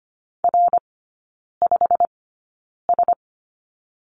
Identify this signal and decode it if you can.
Morse code, heard as a keyed tone, about 25 wpm, spaced as for 9 wpm, 700 Hz. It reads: R5S
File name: course / 25:10wpm / 093 Lesson 16e - Letter c - Three Characters 25wpm